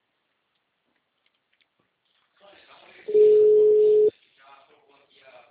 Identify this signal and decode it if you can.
Ringback tone (Europe, 425 Hz), with speech.
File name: out- -2021- - - .52734